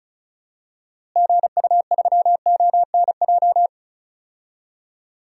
Morse code, heard as a keyed tone, about 36 wpm, 700 Hz. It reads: GU3ONJ
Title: DXCC-100 Call Signs - 6 Characters - Repeated 2x - 35wpm